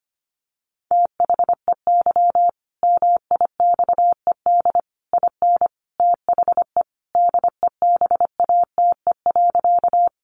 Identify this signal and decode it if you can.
Morse code, heard as a keyed tone, about 25 wpm, 700 Hz. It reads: THEY MIXED IN THE DEBATE.
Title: Sentences from Top 1500 Words 25wpm